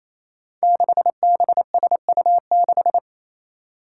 Morse code, heard as a keyed tone, about 27 wpm, 700 Hz. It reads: 6BSU6
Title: Five Challenging Letter-Number Combinations 28wpm